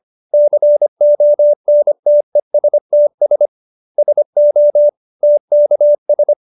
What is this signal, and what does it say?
Morse code, heard as a keyed tone, about 25 wpm, 590 Hz. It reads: CONTESTS SO TKS